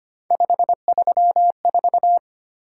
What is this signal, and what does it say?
Morse code, heard as a keyed tone, about 25 wpm, 700 Hz. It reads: 534
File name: Three Letter-Number Combinations - 25wpm